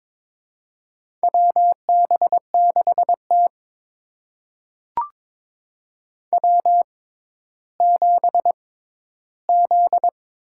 Morse code, heard as a keyed tone, about 22 wpm, spaced as for 12 wpm, 700 Hz. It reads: WB6T W7Z
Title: USA Call Signs - 4 Characters - 6x Character Spacing - 22wpm